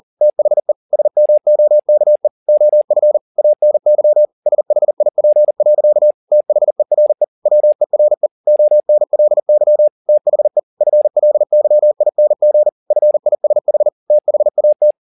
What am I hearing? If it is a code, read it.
Morse code, heard as a keyed tone, about 40 wpm, 600 Hz. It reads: THE SMOKE OF ANY SHIP. THERE WERE ONLY THE FLYING FISH THAT
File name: the old man and the sea 2 40wpm